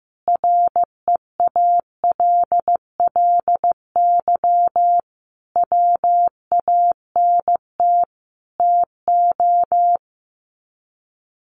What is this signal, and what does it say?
Morse code, heard as a keyed tone, about 15 wpm, 700 Hz. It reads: REALLY WANT TO
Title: Sets of 5 Words Encore2 15wpm